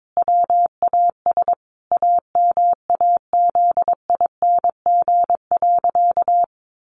Morse code, heard as a keyed tone, about 22 wpm, 700 Hz. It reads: WAS AMAZING.
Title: Sentences from Top 1500 Words 22wpm